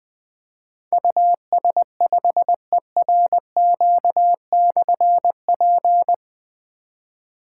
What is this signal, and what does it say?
Morse code, heard as a keyed tone, about 20 wpm, 700 Hz. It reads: US5ERQ/P
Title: Difficult Call Signs 20wpm